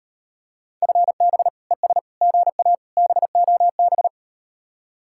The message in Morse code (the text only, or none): FB ES GA BOB